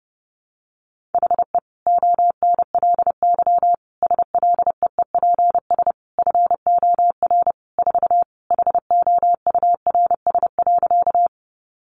Morse code, heard as a keyed tone, about 30 wpm, 700 Hz. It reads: HE ONLY SLEEPS FOR 4 HOURS.
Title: Additional Easy Phrases 30wpm